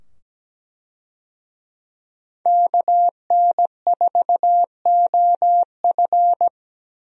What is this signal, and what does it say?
Morse code, heard as a keyed tone, about 17 wpm, 700 Hz. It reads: KN4OF
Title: USA Call Signs - 5 Characters - 17wpm